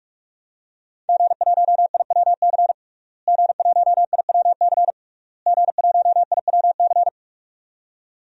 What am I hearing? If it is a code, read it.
Morse code, heard as a keyed tone, about 46 wpm, 700 Hz. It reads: G1IWC G1IWC G1IWC
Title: DXCC-100 Call Signs - 5 Characters - Repeated 3x - 45wpm